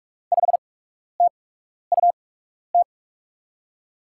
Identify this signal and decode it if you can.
Morse code, heard as a keyed tone, about 44 wpm, spaced as for 12 wpm, 700 Hz. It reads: 5TUT
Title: USA Call Signs - 6 Characters - 8x Character Spacing - 45wpm